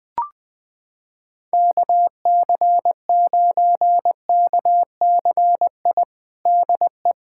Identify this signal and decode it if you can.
Morse code, heard as a keyed tone, about 20 wpm, 700 Hz. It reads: KC9KCI DE
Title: USA Callsign DE Callsign 20wpm